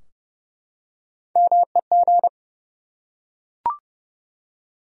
Morse code, heard as a keyed tone, about 30 wpm, 700 Hz. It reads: MEG
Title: USA Common Names - Max 3 Letters - 30wpm